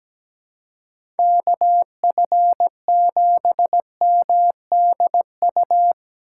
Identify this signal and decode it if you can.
Morse code, heard as a keyed tone, about 17 wpm, 700 Hz. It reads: KF7MDU
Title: USA Call Signs - 6 Characters - 6x Character Spacing - 17wpm